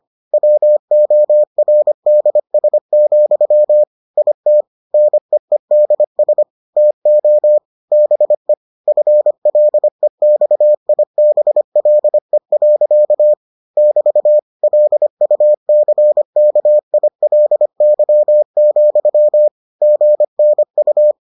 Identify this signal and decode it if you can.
Morse code, heard as a keyed tone, about 25 wpm, 600 Hz. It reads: WORDS, IT NEEDS TO BE FLEXIBLE. <BT> LUCKILY, GNU